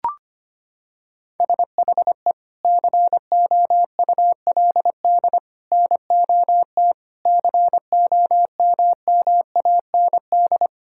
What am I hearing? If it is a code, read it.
Morse code, heard as a keyed tone, about 25 wpm, 700 Hz. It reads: SHE COULD NOT COMMAND
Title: Sentences from Top 1600 Words 25wpm